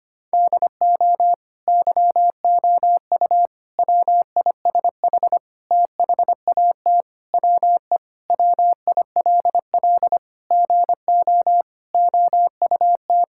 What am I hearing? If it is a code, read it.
Morse code, heard as a keyed tone, about 25 wpm, 700 Hz. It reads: DO YOU WISH THAT WE WILL GO OUT
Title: Sentences from Top 1600 Words 25wpm